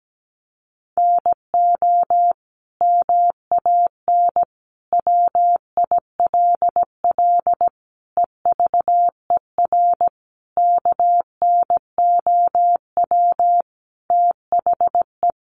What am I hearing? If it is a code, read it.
Morse code, heard as a keyed tone, about 17 wpm, 700 Hz. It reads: NO MAN WILL EVER KNOW THE
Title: Sentences from Top 1300 Words 17wpm